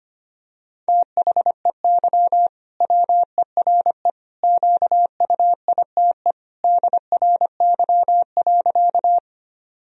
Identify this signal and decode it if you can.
Morse code, heard as a keyed tone, about 25 wpm, 700 Hz. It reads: THEY WERE QUITE DRY.
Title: Sentences from Top 1000 Words - Review Part 3 - 25wpm